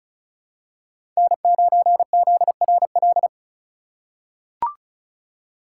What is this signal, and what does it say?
Morse code, heard as a keyed tone, about 34 wpm, 700 Hz. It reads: N9ZRL
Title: USA Call Signs - 5 Characters - 2x Character Spacing - 35wpm